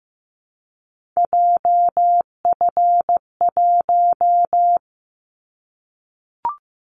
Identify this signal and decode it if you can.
Morse code, heard as a keyed tone, about 15 wpm, 700 Hz. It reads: JF1
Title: Callsign Prefixes for Top 25 DXCC - Repeated 3x - 15wpm